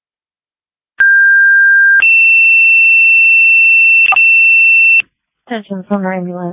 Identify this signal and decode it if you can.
Two-tone page, heard as a one-second tone, then a three-second tone, about 1600 then 2700 Hz, with speech. It is heard from 1.0 to 5.0 s.